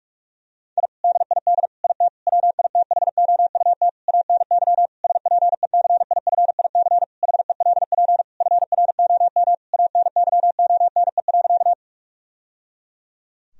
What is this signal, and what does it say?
Morse code, heard as a keyed tone, about 41 wpm, 700 Hz. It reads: I DID IT WITHOUT ANY SPECIFIC HELP FROM ANYONE.
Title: Sentences from Top 1000 Words - Review Part 3 - 45wpm